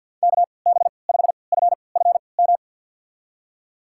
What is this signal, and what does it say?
Morse code, heard as a keyed tone, about 50 wpm, spaced as for 28 wpm, 700 Hz. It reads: KB5FFM
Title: USA Call Signs - 6 Characters - 3x Character Spacing - 50wpm